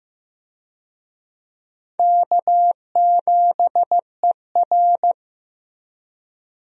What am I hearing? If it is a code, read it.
Morse code, heard as a keyed tone, about 15 wpm, 700 Hz. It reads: K7ER
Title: USA Call Signs - 4 Characters - Repeated 2x - 15wpm